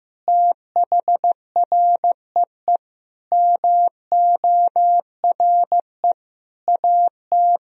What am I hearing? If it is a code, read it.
Morse code, heard as a keyed tone, about 16 wpm, 700 Hz. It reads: THREE MORE AT
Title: Sentences from Top 1000 Words - Review Part 1 - 15wpm